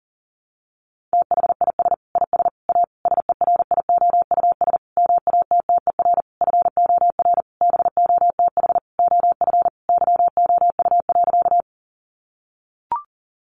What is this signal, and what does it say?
Morse code, heard as a keyed tone, about 40 wpm, 700 Hz. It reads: THIS IS A SERIOUS MATTER FOR BOTH OF YOU.
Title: Sentences from Top 1100 Words 40wpm